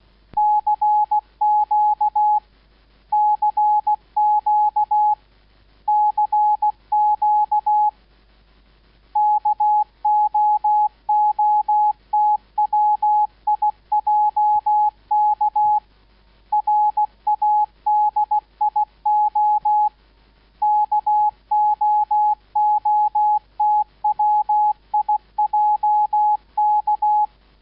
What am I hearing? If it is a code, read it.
Morse code, heard as a keyed tone, about 16 wpm, 840 Hz. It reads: CQ CQ CQ KOOTWIJK RADIO KOOTWIJK